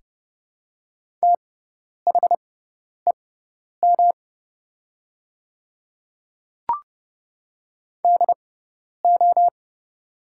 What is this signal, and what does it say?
Morse code, heard as a keyed tone, about 30 wpm, spaced as for 10 wpm, 700 Hz. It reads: THEM DO
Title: 6x Character Spacing - Top 100 Words - 30wpm